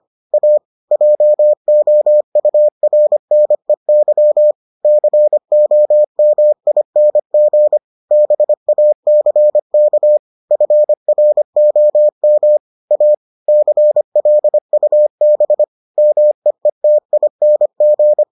Morse code, heard as a keyed tone, about 25 wpm, 600 Hz. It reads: A JOURNEY COMING BACK FROM A CLUB MEETING